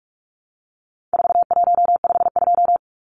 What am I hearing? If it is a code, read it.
Morse code, heard as a keyed tone, about 44 wpm, 700 Hz. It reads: 4152
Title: Numbers - 4-Digit 45wpm